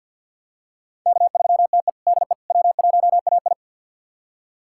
Morse code, heard as a keyed tone, about 51 wpm, 700 Hz. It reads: K3TE DE W1AI
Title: USA Callsign DE Callsign - 4 Characters - 50wpm